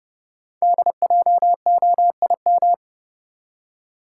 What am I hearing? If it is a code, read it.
Morse code, heard as a keyed tone, about 31 wpm, 690 Hz. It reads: DJOIM
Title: Code Groups - 5 Letters - 30wpm